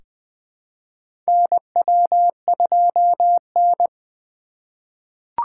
Morse code, heard as a keyed tone, about 20 wpm, 700 Hz. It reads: NW2N
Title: USA Call Signs - 4 Characters - 2x Character Spacing - 20wpm